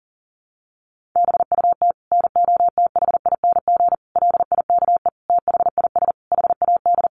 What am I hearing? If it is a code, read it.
Morse code, heard as a keyed tone, about 40 wpm, 700 Hz. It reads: BUT NOTHING LIKE THIS HAD
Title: Sentences from Top 1100 Words 40wpm